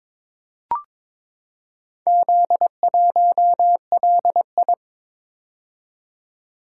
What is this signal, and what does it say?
Morse code, heard as a keyed tone, about 22 wpm, 700 Hz. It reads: Z1LI
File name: Four Letter-Number Combinations - 22wpm